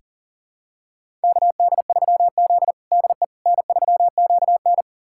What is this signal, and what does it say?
Morse code, heard as a keyed tone, about 37 wpm, 700 Hz. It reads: KD3Z DE N3QN
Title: USA Callsign DE Callsign - 4 Characters - 40wpm